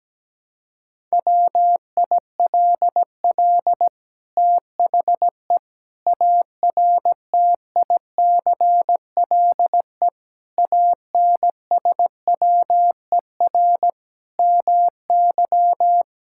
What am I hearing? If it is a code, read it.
Morse code, heard as a keyed tone, about 17 wpm, 700 Hz. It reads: WILL THE ARTICLE ANSWER MY
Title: Sentences from Top 1000 Words - Review Part 5 - 17wpm